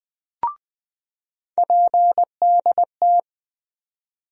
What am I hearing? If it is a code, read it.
Morse code, heard as a keyed tone, about 20 wpm, 700 Hz. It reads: PDT
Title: USA Call Sign Postfixes - 3 Letters 20wpm